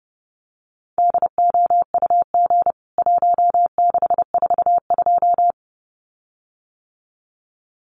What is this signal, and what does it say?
Morse code, heard as a keyed tone, about 30 wpm, 700 Hz. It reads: DOUG 1642
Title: CWT Exchange 30wpm